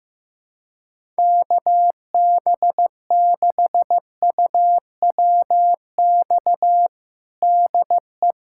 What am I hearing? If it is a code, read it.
Morse code, heard as a keyed tone, about 15 wpm, 700 Hz. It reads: KB6UWX DE
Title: USA Callsign DE Callsign 15wpm